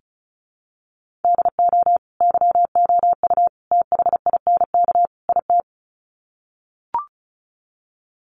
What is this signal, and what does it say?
Morse code, heard as a keyed tone, about 34 wpm, 700 Hz. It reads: DO YOU THINK IT
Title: Sets of 4 Words 35wpm